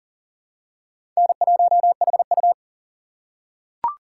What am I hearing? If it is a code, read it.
Morse code, heard as a keyed tone, about 44 wpm, 700 Hz. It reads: N1HU
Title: USA Call Signs - 4 Characters - 6x Character Spacing - 40wpm